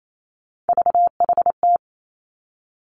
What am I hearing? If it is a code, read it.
Morse code, heard as a keyed tone, about 28 wpm, 700 Hz. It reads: VHT
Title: Three Challenging Letter-Number Combinations 28wpm